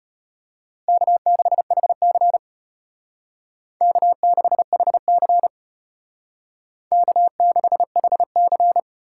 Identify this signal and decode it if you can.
Morse code, heard as a keyed tone, about 32 wpm, 700 Hz. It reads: K6HC K6HC K6HC